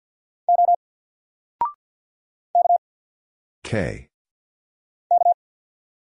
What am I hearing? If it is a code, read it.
Morse code, heard as a keyed tone, about 48 wpm, 700 Hz. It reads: O K K
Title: Single Character (Letter:Number:Punctuation:Prosign) Alt 50wpm